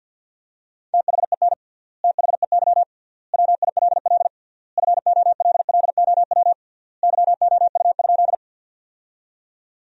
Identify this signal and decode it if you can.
Morse code, heard as a keyed tone, about 51 wpm, 700 Hz. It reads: THEN THEY WILL FOLLOW YOU?
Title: 3x Word-Spacing - Sentences from Top 200 Words - 50wpm